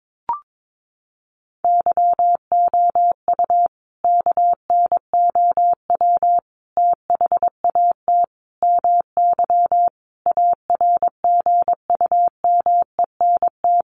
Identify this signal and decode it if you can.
Morse code, heard as a keyed tone, about 22 wpm, 700 Hz. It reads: YOU KNOW THAT MY ARGUMENT
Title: Sentences from Top 1400 Words 22wpm